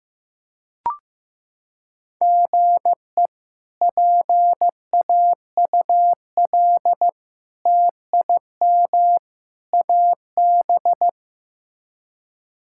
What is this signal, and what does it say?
Morse code, heard as a keyed tone, about 16 wpm, 700 Hz. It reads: GE PAUL TIM AB
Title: K1USN SST QSO Practice - 15wpm